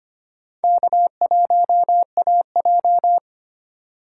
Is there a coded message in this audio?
Yes: Morse code from 0.6 to 3.2 s, heard as a keyed tone, about 33 wpm, 700 Hz. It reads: K1AJ